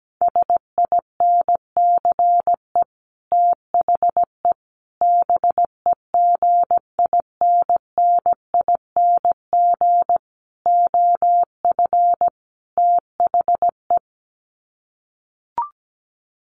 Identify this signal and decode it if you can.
Morse code, heard as a keyed tone, about 17 wpm, 700 Hz. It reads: SINCE THE BEGINNING OF THE
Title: Sets of 5 Words Encore2 17wpm